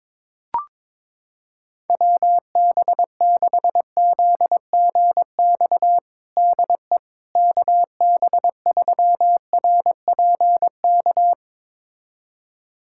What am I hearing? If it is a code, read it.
Morse code, heard as a keyed tone, about 22 wpm, 700 Hz. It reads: WB6ZGX DE KB3RPK
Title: USA Callsign DE Callsign 22wpm